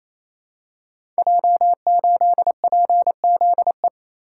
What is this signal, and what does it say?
Morse code, heard as a keyed tone, about 28 wpm, 700 Hz. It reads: J8PZE
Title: Code Groups - 5 Letter:Numbers - 28wpm